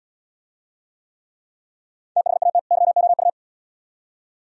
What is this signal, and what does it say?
Morse code, heard as a keyed tone, about 71 wpm, 700 Hz. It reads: THAT YOU